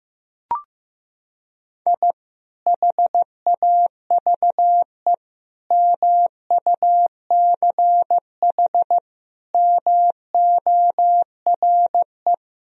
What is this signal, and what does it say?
Morse code, heard as a keyed tone, about 15 wpm, 700 Hz. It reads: I HAVE MUCH MORE
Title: Sentences from Top 1000 Words - Review Part 1 - 15wpm